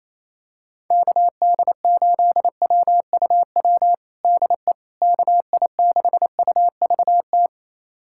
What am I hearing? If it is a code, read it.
Morse code, heard as a keyed tone, about 28 wpm, 700 Hz. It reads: KD8WUW DE KI6UVT
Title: USA Callsign DE Callsign - 6 Characters - 28wpm